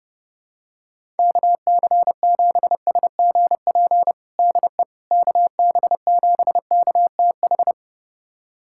Morse code, heard as a keyed tone, about 32 wpm, 700 Hz. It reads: KC7SGP DE KB7KTH
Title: USA Callsign DE Callsign 30wpm